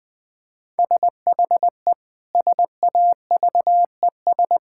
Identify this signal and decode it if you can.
Morse code, heard as a keyed tone, about 20 wpm, 700 Hz. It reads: SHE SAVES
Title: Difficult Sentences with Hints - 20wpm